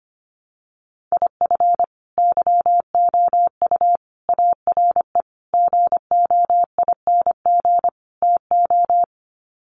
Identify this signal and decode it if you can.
Morse code, heard as a keyed tone, about 25 wpm, 700 Hz. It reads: IF YOU ARE GOING TO
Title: Sets of 5 Words Encore2 25wpm